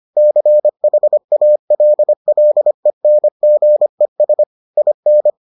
Morse code, heard as a keyed tone, about 25 wpm, 600 Hz. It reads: CHALLENGES IN